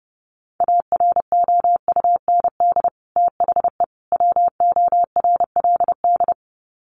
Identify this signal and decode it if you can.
Morse code, heard as a keyed tone, about 28 wpm, 700 Hz. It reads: AROUND THE WORLD
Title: Sets of 3 Words 30wpm